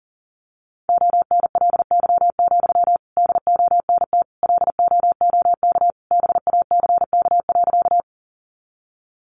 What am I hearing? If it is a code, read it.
Morse code, heard as a keyed tone, about 40 wpm, 700 Hz. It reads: ONLY, DONT LOOK BACK.
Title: Sentences from Top 300 Words 40wpm